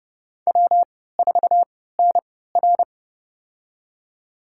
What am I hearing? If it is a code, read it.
Morse code, heard as a keyed tone, about 29 wpm, spaced as for 17 wpm, 700 Hz. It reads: W4NR
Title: USA Call Signs - 4 Characters - 3x Character Spacing - 30wpm